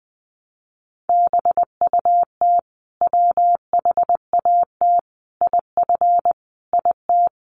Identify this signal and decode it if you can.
Morse code, heard as a keyed tone, about 20 wpm, 700 Hz. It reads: BUT WHAT IF IT
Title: Sentences from Top 100 Words 20wpm